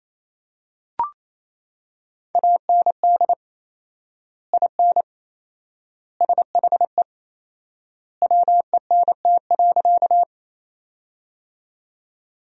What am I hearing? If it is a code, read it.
Morse code, heard as a keyed tone, about 28 wpm, 700 Hz. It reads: AND IN SHE WENT.
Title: 4x Word-Spacing - Sentences from Top 200 Words - 28wpm